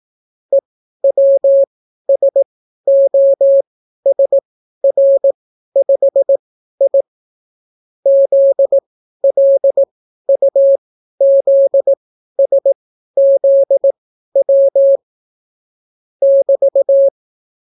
Morse code, heard as a keyed tone, about 18 wpm, spaced as for 12 wpm, 550 Hz. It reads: EWSOSR5I ZLUZSZW <BT>